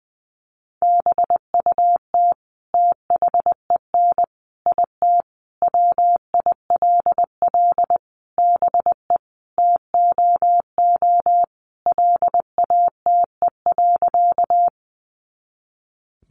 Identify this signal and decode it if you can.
Morse code, heard as a keyed tone, about 20 wpm, 700 Hz. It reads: BUT THEN IT WILL BE TOO LATE.
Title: Sentences from Top 300 Words 20wpm